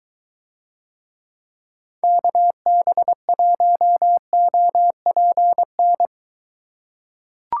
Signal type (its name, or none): Morse code